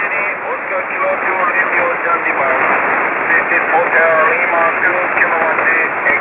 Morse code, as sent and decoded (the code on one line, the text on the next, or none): none